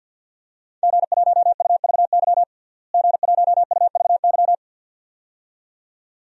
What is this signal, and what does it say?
Morse code, heard as a keyed tone, about 51 wpm, 700 Hz. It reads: G1UVY G1UVY
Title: DXCC-100 Call Signs - 5 Characters - Repeated 2x - 50wpm